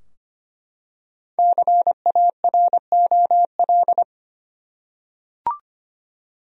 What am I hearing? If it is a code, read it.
Morse code, heard as a keyed tone, about 25 wpm, 700 Hz. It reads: CAROL
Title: USA Common Names - 25wpm